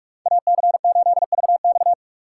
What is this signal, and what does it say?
Morse code, heard as a keyed tone, about 41 wpm, 700 Hz. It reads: AC8VX
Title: USA Call Signs - 3x Character Spacing - 45wpm